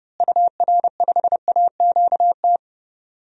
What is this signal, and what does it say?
Morse code, heard as a keyed tone, about 30 wpm, 700 Hz. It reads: UR5AQT
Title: DXCC-100 Call Signs - 6 Characters - Repeated 2x - 30wpm